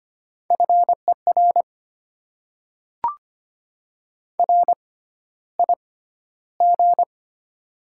Morse code, heard as a keyed tone, about 25 wpm, spaced as for 13 wpm, 690 Hz. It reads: FER RIG
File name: Top 20 QSO Elements - 6x Character Spacing - 25wpm